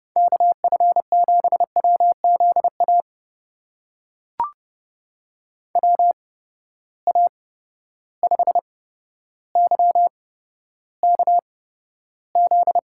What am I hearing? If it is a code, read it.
Morse code, heard as a keyed tone, about 32 wpm, spaced as for 13 wpm, 700 Hz. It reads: KF7WZA WA5YKZ
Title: USA Call Signs - 6 Characters - 8x Character Spacing - 30wpm